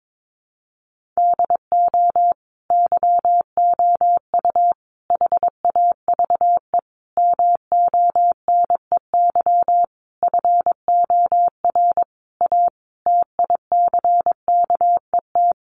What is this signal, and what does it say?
Morse code, heard as a keyed tone, about 22 wpm, 700 Hz. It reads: DO YOU HAVE MONEY FOR A TICKET
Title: Sentences from Top 1400 Words 22wpm